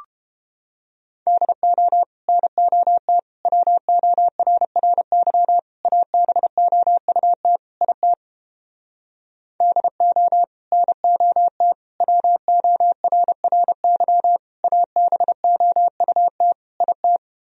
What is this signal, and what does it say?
Morse code, heard as a keyed tone, about 29 wpm, 700 Hz. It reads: DO NOT WORRY ABOUT IT DO NOT WORRY ABOUT IT